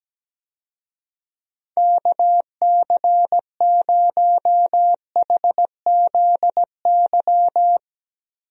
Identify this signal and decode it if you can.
Morse code, heard as a keyed tone, about 17 wpm, 700 Hz. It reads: KC0HZY
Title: USA Call Signs - 6 Characters - Repeated 2x - 17wpm